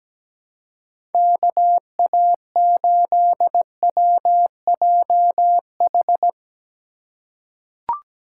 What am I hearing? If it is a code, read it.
Morse code, heard as a keyed tone, about 17 wpm, 700 Hz. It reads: KA8WJH
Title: USA Call Signs - 6 Characters - 3x Character Spacing - 17wpm